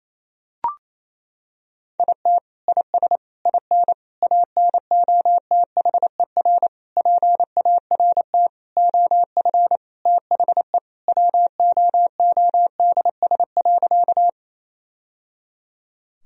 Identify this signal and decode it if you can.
Morse code, heard as a keyed tone, about 28 wpm, 700 Hz. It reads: IT IS IN ANOTHER PART OF THE WOODS.
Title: Sentences from Top 1000 Words - Review Part 2 - 28wpm